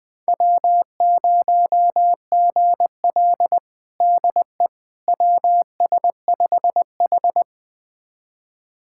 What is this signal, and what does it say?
Morse code, heard as a keyed tone, about 20 wpm, 700 Hz. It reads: W0GL DE WS5H
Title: USA Callsign DE Callsign - 4 Characters - 20wpm